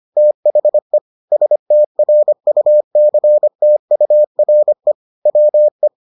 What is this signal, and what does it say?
Morse code, heard as a keyed tone, about 25 wpm, 600 Hz. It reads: THE STRUCTURE WE